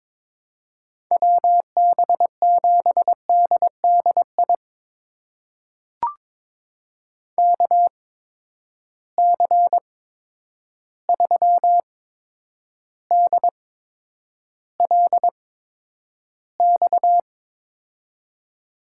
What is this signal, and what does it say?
Morse code, heard as a keyed tone, about 22 wpm, spaced as for 9 wpm, 700 Hz. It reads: WB7DDI KC3DLX
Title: USA Call Signs - 6 Characters - 8x Character Spacing - 22wpm